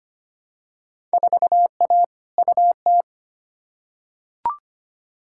Morse code, heard as a keyed tone, about 25 wpm, 700 Hz. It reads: 4A UT
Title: Field Day Exchanges - 25wpm